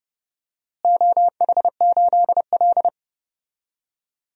Morse code, heard as a keyed tone, about 30 wpm, 700 Hz. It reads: OH8L
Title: DXCC-100 Call Signs - 4 Characters - Repeated 3x - 30wpm